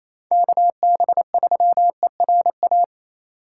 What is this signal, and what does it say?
Morse code, heard as a keyed tone, about 27 wpm, 700 Hz. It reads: KB3ERA
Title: USA Call Signs - 6 Characters - 3x Character Spacing - 28wpm